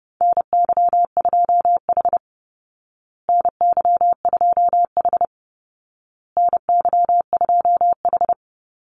Morse code, heard as a keyed tone, about 29 wpm, 700 Hz. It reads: NY2H NY2H NY2H
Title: Call Signs Repeated 2x - 30wpm